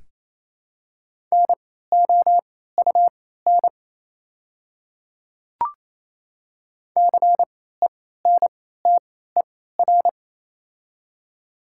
Morse code, heard as a keyed tone, about 28 wpm, spaced as for 16 wpm, 700 Hz. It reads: NOUN CENTER
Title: 3x Character Spacing - Top 500 Words - 28wpm